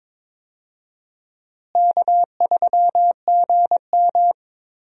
Morse code, heard as a keyed tone, about 22 wpm, 700 Hz. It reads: K3GM
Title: USA Call Signs - 4 Characters - Repeated 3x - 22wpm